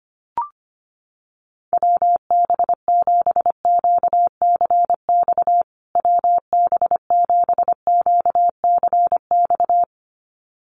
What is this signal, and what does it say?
Morse code, heard as a keyed tone, about 25 wpm, 700 Hz. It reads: WB7QCX WB7QCX
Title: USA Call Signs Repeated 2x 25wpm